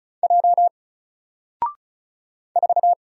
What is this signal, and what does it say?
Morse code, heard as a keyed tone, about 33 wpm, 700 Hz. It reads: J 4
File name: Single-Letter-Number-35wpm